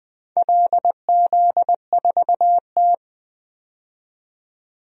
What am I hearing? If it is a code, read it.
Morse code, heard as a keyed tone, about 20 wpm, 700 Hz. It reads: LZ4T
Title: DXCC-100 Call Signs - 4 Characters - 20wpm